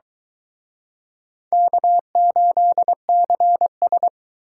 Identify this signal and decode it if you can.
Morse code, heard as a keyed tone, about 23 wpm, 700 Hz. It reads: K8CS